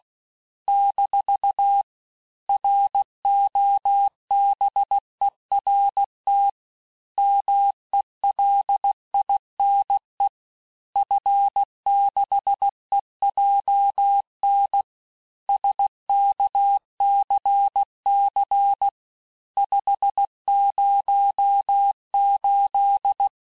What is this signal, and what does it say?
Morse code, heard as a keyed tone, about 16 wpm, 800 Hz. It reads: - ROBERT MELINE F6EJN SKCC 508
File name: cours-du-lundi files / cours-1592--teste-FIN-de-QSO-vitesse-15mpm